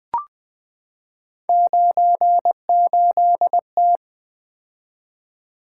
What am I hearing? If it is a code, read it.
Morse code, heard as a keyed tone, about 20 wpm, 700 Hz. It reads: 98T
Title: Three Letter-Number Combinations - 20wpm